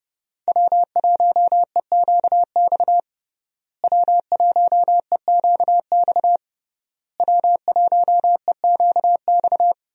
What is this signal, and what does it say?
Morse code, heard as a keyed tone, about 30 wpm, 700 Hz. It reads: W1EQX W1EQX W1EQX
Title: USA Call Signs - 5 Characters - Repeated 3x - 30wpm